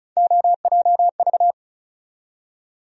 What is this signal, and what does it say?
Morse code, heard as a keyed tone, about 36 wpm, 700 Hz. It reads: OJV
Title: USA Call Sign Postfixes - 3 Letters 35wpm